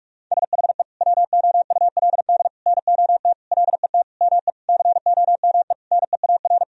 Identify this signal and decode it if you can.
Morse code, heard as a keyed tone, about 41 wpm, 700 Hz. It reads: SHE WOULD NOT LET ME COME NEAR
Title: Sentences from Top 1000 Words - Review Part 1 - 45wpm